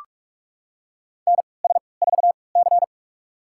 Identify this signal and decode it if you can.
Morse code, heard as a keyed tone, about 46 wpm, spaced as for 26 wpm, 700 Hz. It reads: NS4C
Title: USA Call Signs - 4 Characters - 3x Character Spacing - 45wpm